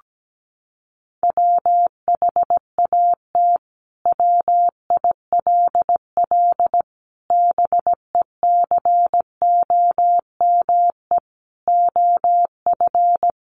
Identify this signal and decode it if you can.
Morse code, heard as a keyed tone, about 17 wpm, 700 Hz. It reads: WHAT WILL BECOME OF
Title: Sentences from Top 600 Words 17wpm